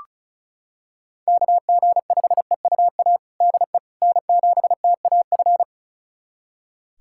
Morse code, heard as a keyed tone, about 36 wpm, 690 Hz. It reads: KG5EUA DE N7TAF
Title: USA Callsign DE Callsign 35wpm